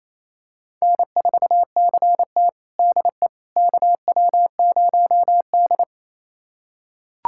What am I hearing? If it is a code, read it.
Morse code, heard as a keyed tone, about 27 wpm, 690 Hz. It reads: N4CT DE KW0D
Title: USA Callsign DE Callsign - 4 Characters - 28wpm